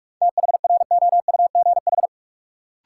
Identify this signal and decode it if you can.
Morse code, heard as a keyed tone, about 43 wpm, 700 Hz. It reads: THROUGH
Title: Sets of 2 Words - New Word Drill - 45wpm